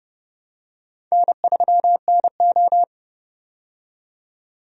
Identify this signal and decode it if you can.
Morse code, heard as a keyed tone, about 29 wpm, 700 Hz. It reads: N3NO
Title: USA Call Signs - 4 Characters - 30wpm